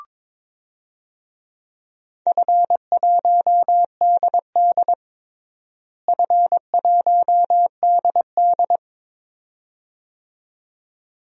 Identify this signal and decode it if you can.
Morse code, heard as a keyed tone, about 22 wpm, 700 Hz. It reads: F1DD F1DD
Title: DXCC-100 Call Signs - 4 Characters - Repeated 2x - 22wpm